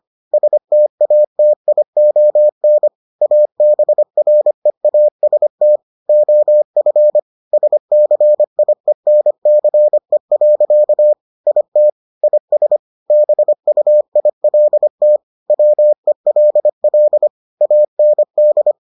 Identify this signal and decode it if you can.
Morse code, heard as a keyed tone, about 25 wpm, 600 Hz. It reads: STATION ABREAST OF SCIENCE. IT IS BUILT WELL AND